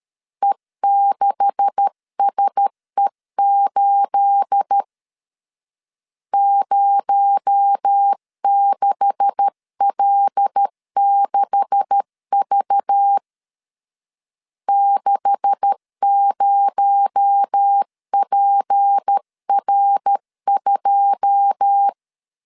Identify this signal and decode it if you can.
Morse code, heard as a keyed tone, about 13 wpm, 800 Hz. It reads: E6SE8 06L6V 60PR2
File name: LEZIONI / 9 286 12wpm 2305